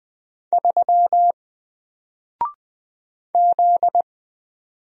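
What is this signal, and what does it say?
Morse code, heard as a keyed tone, about 20 wpm, 700 Hz. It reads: 3 Z
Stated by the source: Single letter and number Morse drill at 20 wpm, from N4LCD.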